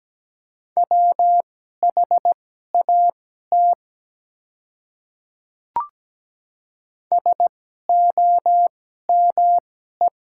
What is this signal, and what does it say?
Morse code, heard as a keyed tone, about 17 wpm, spaced as for 12 wpm, 700 Hz. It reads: WHAT SOME
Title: 2x Character Spacing - Top 200 Words - 17wpm